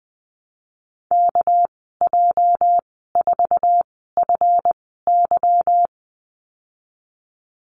Morse code, heard as a keyed tone, about 20 wpm, spaced as for 15 wpm, 700 Hz. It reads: KJ4FY
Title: 207 Lesson 33g - Letter j - Call Signs 20wpm